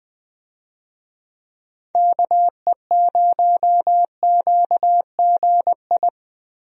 Morse code, heard as a keyed tone, about 20 wpm, 700 Hz. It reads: KE0QGI
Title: USA Call Signs - 6 Characters - Repeated 2x - 20wpm